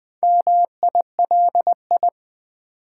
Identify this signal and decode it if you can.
Morse code, heard as a keyed tone, about 20 wpm, 700 Hz. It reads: MILI
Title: Build A Word - 4 Letters At A Time - 8-Letter Words - 20wpm